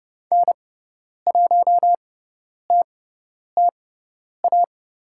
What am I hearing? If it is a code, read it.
Morse code, heard as a keyed tone, about 30 wpm, spaced as for 10 wpm, 700 Hz. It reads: N1TTA
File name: Lesson 16g - Letter c - Call Signs 30wpm